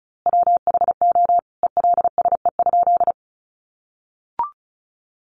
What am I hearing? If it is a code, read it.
Morse code, heard as a keyed tone, about 34 wpm, 700 Hz. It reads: WHO ELSE?
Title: A Sound Of Thunder 35wpm